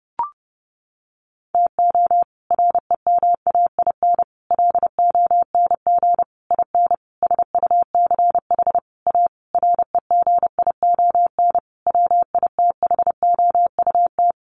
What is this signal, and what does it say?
Morse code, heard as a keyed tone, about 30 wpm, 700 Hz. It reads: TO REMAIN LONG IN SUCH A REGION WITHOUT
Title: Sentences from Top 1500 Words 30wpm